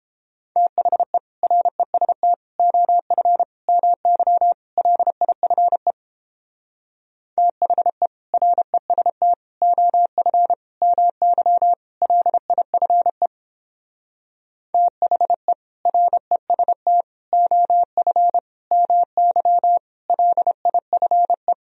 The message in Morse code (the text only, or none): THE REST OF MY LIFE THE REST OF MY LIFE THE REST OF MY LIFE